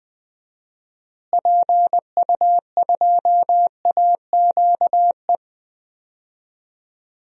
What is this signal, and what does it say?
Morse code, heard as a keyed tone, about 21 wpm, 700 Hz. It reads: PU2AQE
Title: DXCC-100 Call Signs - 6 Characters - Repeated 2x - 20wpm